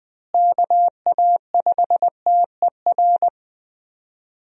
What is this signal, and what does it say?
Morse code, heard as a keyed tone, about 20 wpm, 700 Hz. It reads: KA5TER
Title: USA Call Signs - 6 Characters - Repeated 3x - 20wpm